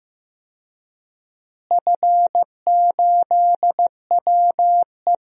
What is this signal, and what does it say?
Morse code, heard as a keyed tone, about 15 wpm, 700 Hz. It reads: F8WE
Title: DXCC-100 Call Signs - 4 Characters - Repeated 2x - 15wpm